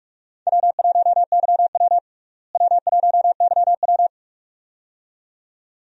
Morse code, heard as a keyed tone, about 46 wpm, 700 Hz. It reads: W1YW W1YW
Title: USA Call Signs - 4 Characters - Repeated 2x - 45wpm